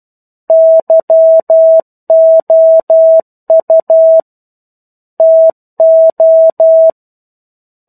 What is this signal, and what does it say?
Morse code, heard as a keyed tone, about 12 wpm, 650 Hz. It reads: YOU TO
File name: 01-Chapter-01 12 WPM